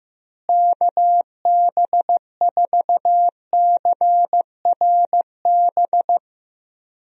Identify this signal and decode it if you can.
Morse code, heard as a keyed tone, about 15 wpm, 700 Hz. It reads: KB4CRB
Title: USA Call Signs - 6 Characters - Repeated 2x - 15wpm